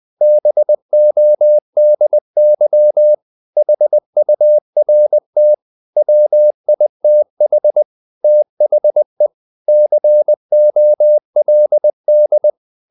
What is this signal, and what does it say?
Morse code, heard as a keyed tone, about 20 wpm, 600 Hz. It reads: BODY HURT WITH THE COLD
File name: the old man and the sea part 2 20wpm